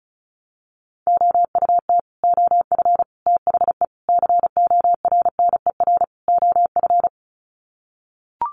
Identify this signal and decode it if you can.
Morse code, heard as a keyed tone, about 34 wpm, 700 Hz. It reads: OUT OF THE CORNER OF
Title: Sets of 5 Words 35wpm